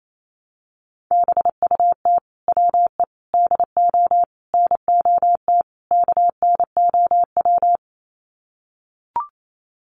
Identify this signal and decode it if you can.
Morse code, heard as a keyed tone, about 28 wpm, 700 Hz. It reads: BUT WE DO NOT KNOW
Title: Sets of 5 Words Encore2 28wpm